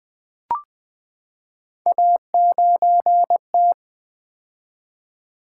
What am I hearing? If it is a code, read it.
Morse code, heard as a keyed tone, about 20 wpm, 700 Hz. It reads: A9T
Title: Three Letter-Number Combinations - 20wpm